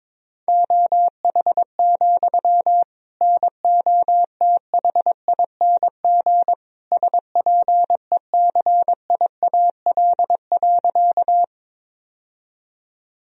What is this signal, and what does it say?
Morse code, heard as a keyed tone, about 22 wpm, 700 Hz. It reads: OH, NOTHING SPECIAL.
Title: Sentences from Top 1000 Words - Review Part 3 - 22wpm